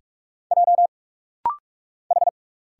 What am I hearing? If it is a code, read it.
Morse code, heard as a keyed tone, about 46 wpm, 700 Hz. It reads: J H
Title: Single Letter-Number Alt - Rapid-Fire 45wpm